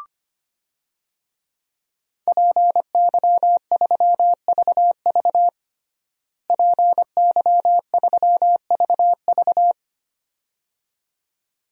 Morse code, heard as a keyed tone, about 25 wpm, 700 Hz. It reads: PY3VV PY3VV
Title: DXCC-100 Call Signs - 5 Characters - Repeated 2x - 25wpm